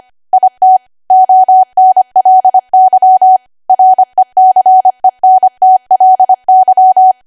Morse code, heard as a keyed tone, about 25 wpm, 750 Hz. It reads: IT ONLY RECENTLY
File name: Archive / 25 WPM / 25WPM